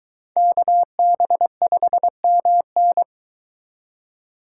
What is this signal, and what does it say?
Morse code, heard as a keyed tone, about 22 wpm, 700 Hz. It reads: KB5MN